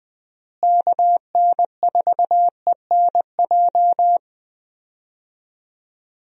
Morse code, heard as a keyed tone, about 20 wpm, 700 Hz. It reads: KN4ENJ
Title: USA Call Signs - 6 Characters - Repeated 2x - 20wpm